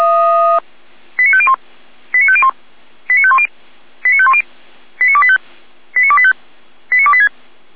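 Five-tone selective call (ZVEI), heard as quick runs of five tones, tones of about 70 ms. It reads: 89501 89501 89510 89510 89285 89285 89286
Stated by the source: ZVEI five-tone call 89280 received via SDR.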